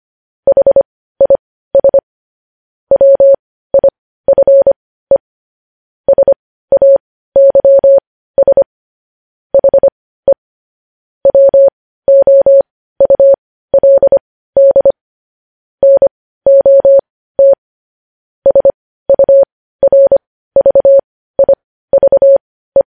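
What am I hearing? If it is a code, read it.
Morse code, heard as a keyed tone, about 25 wpm, spaced as for 15 wpm, 550 Hz. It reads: HIS WIFE SAYS HE WOULD NOT SURVIVE
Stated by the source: News headlines sent in Morse code at 15 wpm.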